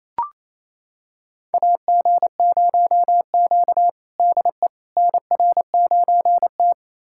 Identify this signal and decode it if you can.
Morse code, heard as a keyed tone, about 28 wpm, 700 Hz. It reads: AG0Q DE NR9T
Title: USA Callsign DE Callsign - 4 Characters - 28wpm